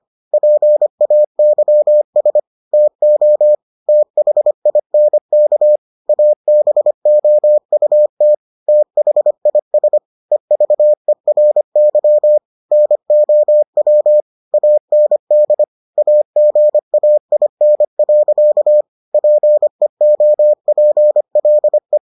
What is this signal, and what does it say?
Morse code, heard as a keyed tone, about 25 wpm, 600 Hz. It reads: PAYS TO THINK ABOUT THIS EVERY NOW AND AGAIN. PEOPLE